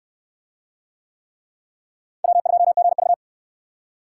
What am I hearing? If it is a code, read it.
Morse code, heard as a keyed tone, about 65 wpm, 700 Hz. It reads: W2GV